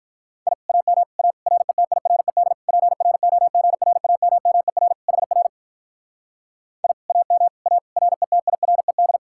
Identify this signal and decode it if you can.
Morse code, heard as a keyed tone, about 46 wpm, 700 Hz. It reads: I AM A RETIRED PROGRAMMER HR I AM A RETIRED